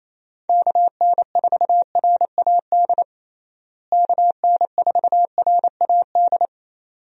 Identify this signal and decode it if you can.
Morse code, heard as a keyed tone, about 27 wpm, 700 Hz. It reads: KN4RAD KN4RAD
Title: USA Call Signs - 6 Characters - Repeated 2x - 28wpm